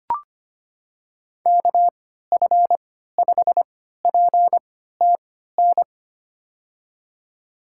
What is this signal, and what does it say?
Morse code, heard as a keyed tone, about 25 wpm, spaced as for 14 wpm, 700 Hz. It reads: KF5PTN
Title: USA Call Signs - 6 Characters - 3x Character Spacing - 25wpm